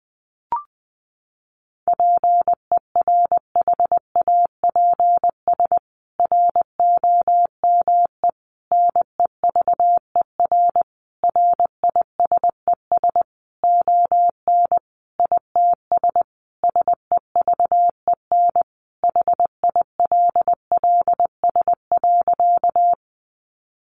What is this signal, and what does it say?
Morse code, heard as a keyed tone, about 20 wpm, 700 Hz. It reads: PERHAPS ROME NEVER RISES ON ITS SEVEN HILLS.
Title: A Sound Of Thunder 20wpm